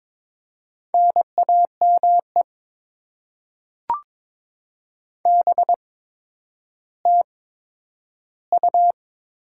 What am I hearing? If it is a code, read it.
Morse code, heard as a keyed tone, about 22 wpm, spaced as for 11 wpm, 700 Hz. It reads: NAME BTU
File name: Top 20 QSO Elements - 8x Character Spacing - 22wpm